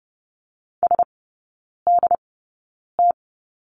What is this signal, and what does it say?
Morse code, heard as a keyed tone, about 30 wpm, spaced as for 9 wpm, 700 Hz. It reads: SDT
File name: 081 Lesson 14e - Number 2 - Three Characters 30wpm